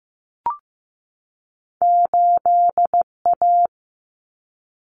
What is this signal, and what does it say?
Morse code, heard as a keyed tone, about 15 wpm, 700 Hz. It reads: 8A
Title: Common Cut Numbers - 2-Digit 15wpm